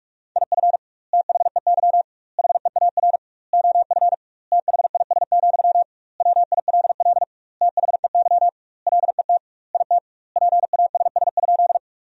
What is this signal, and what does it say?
Morse code, heard as a keyed tone, about 44 wpm, 700 Hz. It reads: IF THEY HEAR OF THIS, WILL THEY LET IT PASS?
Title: 2x Word-Spacing - Sentences from Top 400 Words - 45wpm